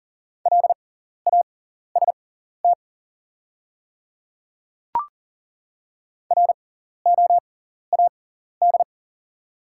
Morse code, heard as a keyed tone, about 37 wpm, spaced as for 14 wpm, 690 Hz. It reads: LAST ROAD
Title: 6x Character Spacing - Top 500 Words - 40wpm